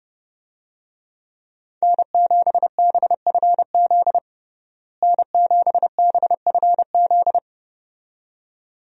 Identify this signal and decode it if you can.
Morse code, heard as a keyed tone, about 32 wpm, 700 Hz. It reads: N7BFZ N7BFZ
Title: USA Call Signs - 5 Characters - Repeated 2x - 30wpm